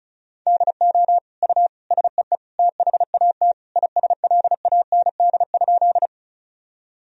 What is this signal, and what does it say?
Morse code, heard as a keyed tone, about 34 wpm, 700 Hz. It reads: DO U SEE THAT ISLAND?
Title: Sentences from Top 500 Words and Abbreviations 35wpm